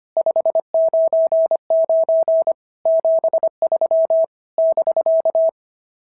Morse code, heard as a keyed tone, about 25 wpm, 690 Hz. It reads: 599 73 <BK>
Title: SOTA QSO Practice 25wpm